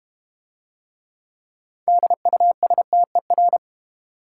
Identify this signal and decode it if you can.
Morse code, heard as a keyed tone, about 31 wpm, 690 Hz. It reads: DUSTER
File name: Easy Words - Speed-Racing with Hints - 25wpm